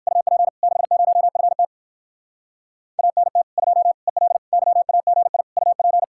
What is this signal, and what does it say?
Morse code, heard as a keyed tone, about 62 wpm, 700 Hz. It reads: UP 60FT ANT 2 EL YAGI UP